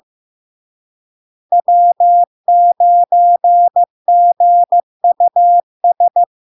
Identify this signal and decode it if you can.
Morse code, heard as a keyed tone, about 15 wpm, 700 Hz. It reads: W9GUS
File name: LARK-FD-2022-Log-in-CW-15wpm-700-Hz